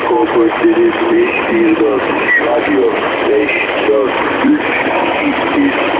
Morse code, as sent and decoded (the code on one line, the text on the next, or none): none